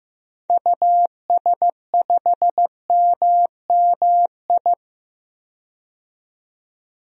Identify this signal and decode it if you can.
Morse code, heard as a keyed tone, about 15 wpm, 700 Hz. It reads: US5MMI
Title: DXCC-100 Call Signs - 6 Characters - 15wpm